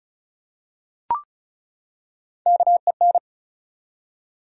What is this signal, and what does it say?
Morse code, heard as a keyed tone, about 36 wpm, 700 Hz. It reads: KEN
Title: USA Common Names - Max 4 Letters - 35wpm